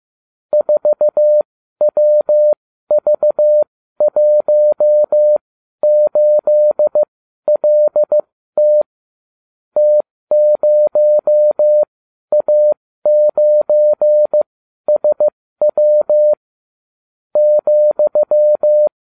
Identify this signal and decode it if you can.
Morse code, heard as a keyed tone, about 15 wpm, spaced as for 12 wpm, 600 Hz. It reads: 4WV18LT T0A9SW ,